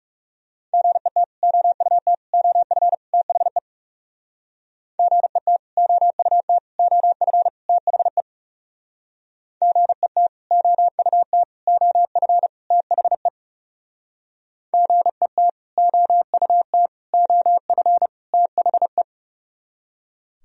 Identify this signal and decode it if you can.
Morse code, heard as a keyed tone, about 38 wpm, 690 Hz. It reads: GET OUT OF THE GET OUT OF THE GET OUT OF THE GET OUT OF THE